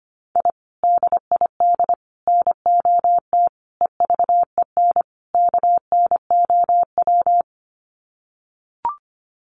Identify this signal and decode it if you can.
Morse code, heard as a keyed tone, about 25 wpm, 700 Hz. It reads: I DID NOT EVEN KNOW
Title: Sets of 5 Words Encore 25wpm